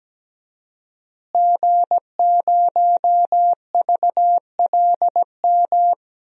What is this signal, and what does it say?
Morse code, heard as a keyed tone, about 17 wpm, 700 Hz. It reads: G0VLM
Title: DXCC-100 Call Signs - 5 Characters - Repeated 2x - 17wpm